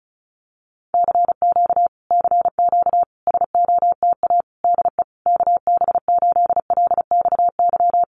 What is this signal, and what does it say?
Morse code, heard as a keyed tone, about 34 wpm, 700 Hz. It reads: CQ CQ SOTA DE KB8LXY
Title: Common QSO Phrases Encore 35wpm